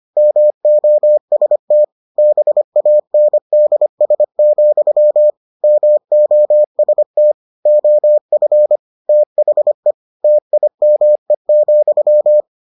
Morse code, heard as a keyed tone, about 25 wpm, 600 Hz. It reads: MOST BANDS, MOST OF THE TIME,